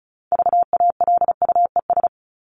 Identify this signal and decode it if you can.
Morse code, heard as a keyed tone, about 34 wpm, 700 Hz. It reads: VALUES
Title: ES:S Suffix - Top 100 Words with Base Words 35wpm